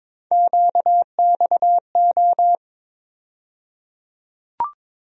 Morse code, heard as a keyed tone, about 22 wpm, 700 Hz. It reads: QXO
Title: USA Call Sign Postfixes - 3 Letters 22wpm